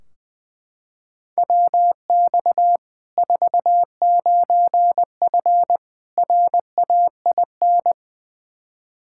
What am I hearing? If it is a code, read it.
Morse code, heard as a keyed tone, about 21 wpm, 700 Hz. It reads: WX 49F RAIN
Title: Practice QSO - Exchange 2 - Info - 20wpm